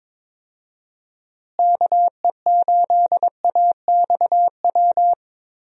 Morse code, heard as a keyed tone, about 22 wpm, 700 Hz. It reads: KE8AXW
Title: USA Call Signs - 6 Characters - Repeated 3x - 22wpm